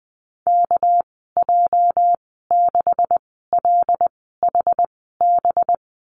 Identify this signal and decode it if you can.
Morse code, heard as a keyed tone, about 20 wpm, spaced as for 15 wpm, 700 Hz. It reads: KJ6LHB
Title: 219 Lesson 35g - Number 0 - Call Signs 20wpm